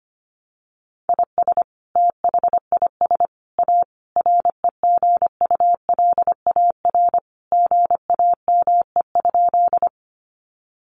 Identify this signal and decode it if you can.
Morse code, heard as a keyed tone, about 25 wpm, 700 Hz. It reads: IS THIS A REGULAR GAME?
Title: Sentences from Top 900 Words 25wpm